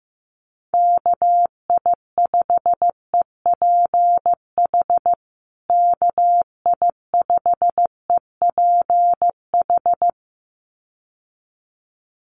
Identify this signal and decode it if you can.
Morse code, heard as a keyed tone, about 15 wpm, 700 Hz. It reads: KI5EPH KI5EPH
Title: USA Call Signs Repeated 2x 15wpm